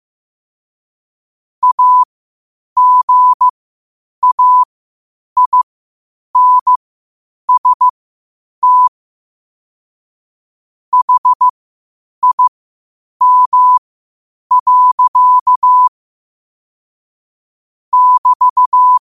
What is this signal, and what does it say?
Morse code, heard as a keyed tone, about 15 wpm, spaced as for 8 wpm, 1000 Hz. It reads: AGAINST HIM. <BT>
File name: qotdfiles / 8wpm